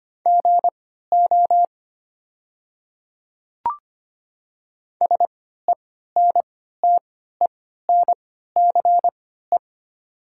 Morse code, heard as a keyed tone, about 25 wpm, spaced as for 14 wpm, 700 Hz. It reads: GO SENTENCE
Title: 3x Character Spacing - Top 300 Words - 25wpm